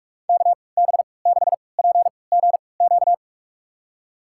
Morse code, heard as a keyed tone, about 46 wpm, spaced as for 26 wpm, 700 Hz. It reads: KB6PGQ
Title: USA Call Signs - 6 Characters - 3x Character Spacing - 45wpm